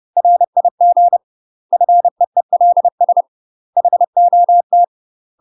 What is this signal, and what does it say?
Morse code, heard as a keyed tone, about 30 wpm, 700 Hz. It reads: RIG FEELS HOT
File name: UK MASTER QSOs TRAINING / 30WPM / UK MASTER QSOs 0022